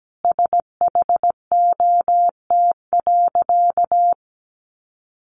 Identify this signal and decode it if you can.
Morse code, heard as a keyed tone, about 17 wpm, 700 Hz. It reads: SHOT.
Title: Sentences from Top 1100 Words 17wpm